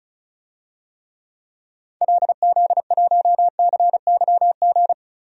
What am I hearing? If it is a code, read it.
Morse code, heard as a keyed tone, about 36 wpm, 700 Hz. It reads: LZ1CYG
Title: DXCC-100 Call Signs - 6 Characters - Repeated 2x - 35wpm